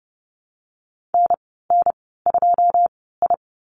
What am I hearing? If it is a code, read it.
Morse code, heard as a keyed tone, about 30 wpm, spaced as for 17 wpm, 700 Hz. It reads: NN2I
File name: Lesson 15g - Number 5 - Call Signs 30wpm